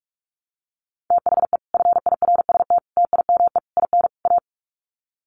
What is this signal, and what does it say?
Morse code, heard as a keyed tone, about 44 wpm, 700 Hz. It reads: THE FIRST TIME IN A
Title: Sets of 5 Words Encore 45wpm